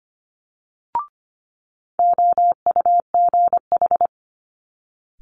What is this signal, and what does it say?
Morse code, heard as a keyed tone, about 25 wpm, 700 Hz. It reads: OUGH
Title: Multi-letter Phonemes Quick Fire 25wpm